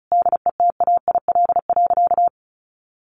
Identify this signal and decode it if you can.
Morse code, heard as a keyed tone, about 34 wpm, 700 Hz. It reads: DETAIL.
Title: Sentences from Top 1500 Words 35wpm